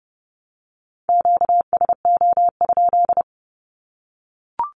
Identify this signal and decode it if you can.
Morse code, heard as a keyed tone, about 30 wpm, 700 Hz. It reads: QSO?
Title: Q-Codes 30wpm